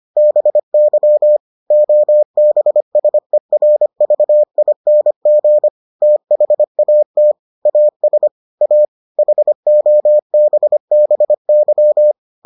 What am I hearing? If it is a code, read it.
Morse code, heard as a keyed tone, about 25 wpm, 600 Hz. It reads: BY OBSERVING THAT AS A HOBBY